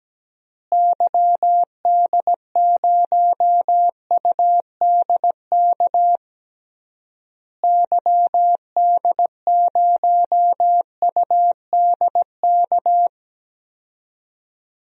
Morse code, heard as a keyed tone, about 17 wpm, 700 Hz. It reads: YD0UDK YD0UDK
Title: DXCC-100 Call Signs - 6 Characters - Repeated 3x - 17wpm